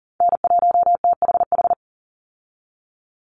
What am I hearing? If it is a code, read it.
Morse code, heard as a keyed tone, about 40 wpm, 700 Hz. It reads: N1THH
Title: USA Call Signs 40wpm